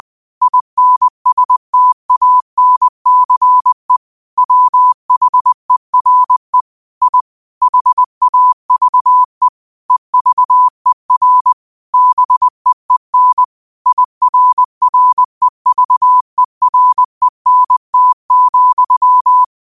Morse code, heard as a keyed tone, about 20 wpm, 1000 Hz. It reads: INSTANCE WHERE I HAVE EVER BEEN IRREVERENT,